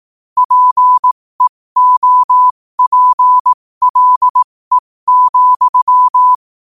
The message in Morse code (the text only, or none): PEOPLE,